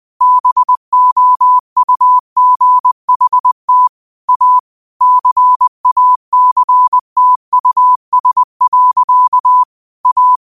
Morse code, heard as a keyed tone, about 20 wpm, 1000 Hz. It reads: BOUGHT A CACTUS. A
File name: qotdfiles / 20wpm